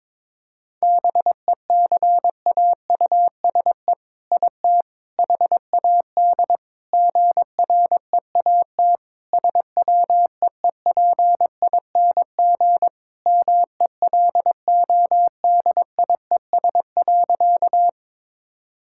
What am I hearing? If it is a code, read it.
Morse code, heard as a keyed tone, about 22 wpm, 700 Hz. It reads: BECAUSE IT HAD GREAT SWEEPING MELODIES.
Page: morse-code-ninja-practice-22wpm